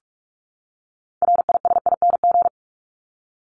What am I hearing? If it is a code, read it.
Morse code, heard as a keyed tone, about 44 wpm, 700 Hz. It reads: RISING
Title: ING Suffix - Top 100 Words with Base Words 45wpm